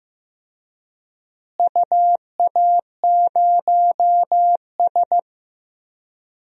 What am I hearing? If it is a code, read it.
Morse code, heard as a keyed tone, about 15 wpm, 700 Hz. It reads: UA0S
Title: DXCC-100 Call Signs - 4 Characters - Repeated 3x - 15wpm